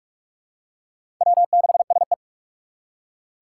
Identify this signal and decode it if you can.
Morse code, heard as a keyed tone, about 46 wpm, 700 Hz. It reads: W6SE
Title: USA Call Signs - 4 Characters - 3x Character Spacing - 45wpm